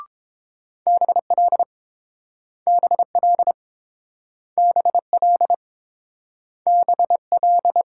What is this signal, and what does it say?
Morse code, heard as a keyed tone, about 30 wpm, 700 Hz. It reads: BL BL BL BL